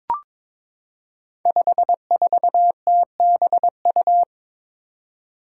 Morse code, heard as a keyed tone, about 22 wpm, 700 Hz. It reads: 54TBU
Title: Five Challenging Letter-Number Combinations 22wpm